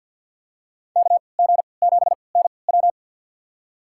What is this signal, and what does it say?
Morse code, heard as a keyed tone, about 51 wpm, spaced as for 28 wpm, 700 Hz. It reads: KG7NW
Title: USA Call Signs - 3x Character Spacing - 50wpm